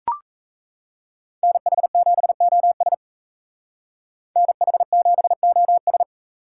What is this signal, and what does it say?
Morse code, heard as a keyed tone, about 34 wpm, 700 Hz. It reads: NH7OS NH7OS
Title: Speed Racing - USA 2x2 Call Signs 28wpm